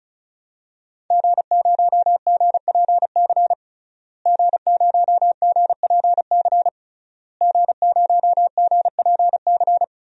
Morse code, heard as a keyed tone, about 36 wpm, 700 Hz. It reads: G0GPC G0GPC G0GPC
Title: DXCC-100 Call Signs - 5 Characters - Repeated 3x - 35wpm